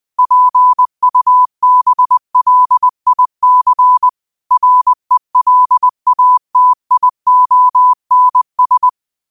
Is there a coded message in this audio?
Yes: Morse code from 0.2 to 8.9 s, heard as a keyed tone, about 20 wpm, 1000 Hz. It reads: PUBLIC RELATIONS